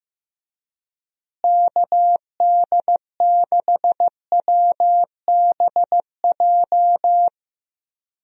Morse code, heard as a keyed tone, about 15 wpm, 700 Hz. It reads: KD6WBJ
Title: USA Call Signs - 6 Characters - Repeated 3x - 15wpm